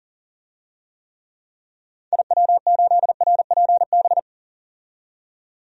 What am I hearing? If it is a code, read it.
Morse code, heard as a keyed tone, about 40 wpm, 700 Hz. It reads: IW8RPB
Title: DXCC-100 Call Signs - 6 Characters - 40wpm